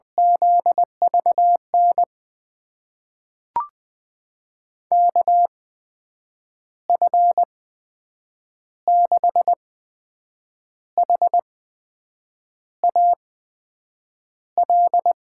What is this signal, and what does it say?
Morse code, heard as a keyed tone, about 20 wpm, spaced as for 7 wpm, 700 Hz. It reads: ZVN KF6HAL
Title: USA Call Signs - 6 Characters - 8x Character Spacing - 20wpm